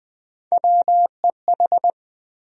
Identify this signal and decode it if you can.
Morse code, heard as a keyed tone, about 21 wpm, 700 Hz. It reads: WEH
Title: USA Call Sign Postfixes - 3 Letters 20wpm